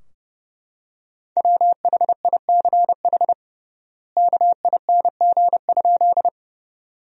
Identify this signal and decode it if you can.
Morse code, heard as a keyed tone, about 32 wpm, 700 Hz. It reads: WHICH KING?
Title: 3x Word-Spacing - Sentences from Top 400 Words - 30wpm